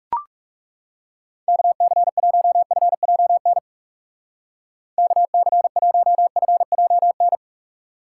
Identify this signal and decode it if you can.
Morse code, heard as a keyed tone, about 36 wpm, 700 Hz. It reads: KC1FJN KC1FJN
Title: Speed Racing - USA 2x3 Call Signs 30wpm